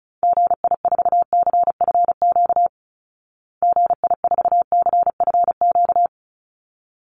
Morse code, heard as a keyed tone, about 34 wpm, 700 Hz. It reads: GI4CFQ GI4CFQ
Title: Call Signs Repeated 3x - 35wpm